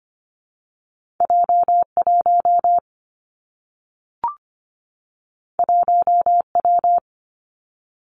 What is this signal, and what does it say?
Morse code, heard as a keyed tone, about 24 wpm, 700 Hz. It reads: J 1 1 W
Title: Two Challenging Letter-Number Combinations 25wpm